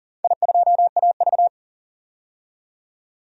Morse code, heard as a keyed tone, about 40 wpm, 700 Hz. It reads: I2AV
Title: DXCC-100 Call Signs - 4 Characters - 40wpm